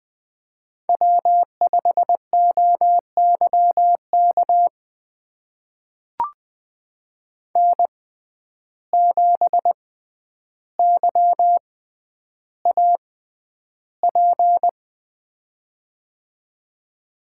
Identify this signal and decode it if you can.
Morse code, heard as a keyed tone, about 20 wpm, spaced as for 10 wpm, 700 Hz. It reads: W5OYK N7YAP
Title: USA Call Signs - 5 Characters - 6x Character Spacing - 20wpm